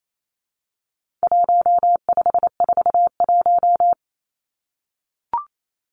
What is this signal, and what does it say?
Morse code, heard as a keyed tone, about 28 wpm, 700 Hz. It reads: 1541
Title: Numbers - 4-Digit 28wpm